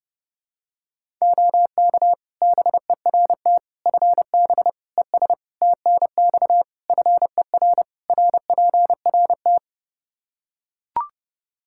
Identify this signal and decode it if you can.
Morse code, heard as a keyed tone, about 30 wpm, 700 Hz. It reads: OK BERT FB ES TNX FER RPRT
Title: Practice QSO - Exchange 2 - Info - 30wpm